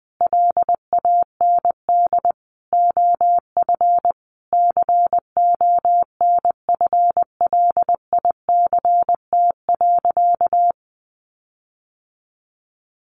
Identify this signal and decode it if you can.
Morse code, heard as a keyed tone, about 20 wpm, 700 Hz. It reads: LAND OF CONFLICT.
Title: Sentences from Top 1300 Words 20wpm